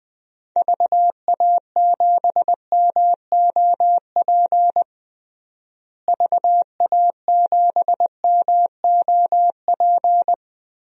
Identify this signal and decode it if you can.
Morse code, heard as a keyed tone, about 20 wpm, 700 Hz. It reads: VA7MOP VA7MOP
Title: DXCC-100 Call Signs - 6 Characters - Repeated 3x - 20wpm